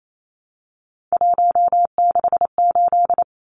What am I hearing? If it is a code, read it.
Morse code, heard as a keyed tone, about 28 wpm, 700 Hz. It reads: 168
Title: Numbers - 3-Digit 28wpm